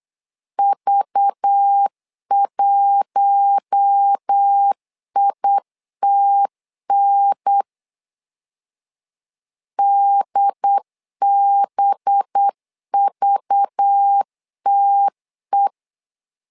Morse code, heard as a keyed tone, about 8 wpm, 800 Hz. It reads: V1ITN DBVTE
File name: LEZIONI / 5 LBVC / 5 LBV 8wpm 2837